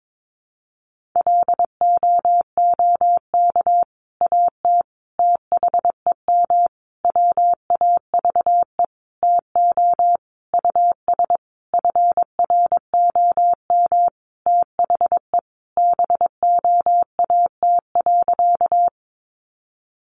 Morse code, heard as a keyed tone, about 22 wpm, 700 Hz. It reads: LOOK AT THEM WAVE TO US FROM THE BOAT.
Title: Sentences from Top 1400 Words 22wpm